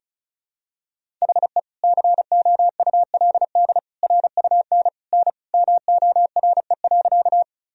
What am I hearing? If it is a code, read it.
Morse code, heard as a keyed tone, about 36 wpm, 700 Hz. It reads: HE COULD RUN N MORE.
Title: Sentences from Top 300 Words and Abbreviations 35wpm